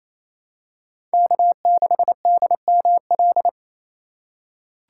Morse code, heard as a keyed tone, about 27 wpm, 700 Hz. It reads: K6DML
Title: USA Call Signs - 5 Characters - 4x Character Spacing - 28wpm